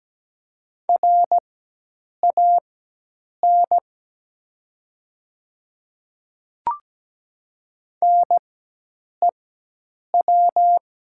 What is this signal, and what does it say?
Morse code, heard as a keyed tone, about 17 wpm, spaced as for 8 wpm, 700 Hz. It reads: RAN NEW
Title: 4x Character Spacing - Top 500 Words - 17wpm